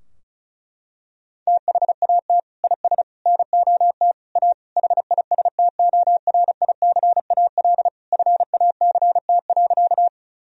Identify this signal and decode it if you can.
Morse code, heard as a keyed tone, about 36 wpm, 700 Hz. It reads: THAT IS NOT A HISTORICAL FACT.
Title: Sentences from Top 1000 Words - Review Part 4 - 35wpm